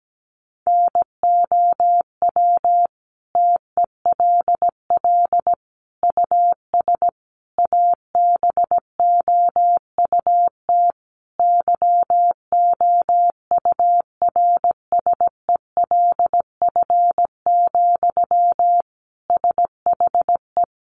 Morse code, heard as a keyed tone, about 17 wpm, 700 Hz. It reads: NOW TELL US ABOUT YOURSELF, SHE
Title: Sentences from Top 600 Words 17wpm